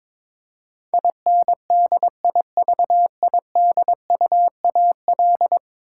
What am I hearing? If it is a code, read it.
Morse code, heard as a keyed tone, about 22 wpm, 700 Hz. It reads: INDIVIDUAL
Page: morse-code-ninja-practice-22wpm